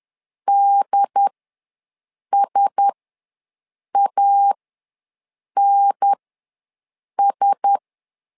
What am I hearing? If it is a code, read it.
Morse code, heard as a keyed tone, about 10 wpm, spaced as for 6 wpm, 800 Hz. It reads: DSANS
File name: LEZIONI / 4 Q1S ws 10wpm 2208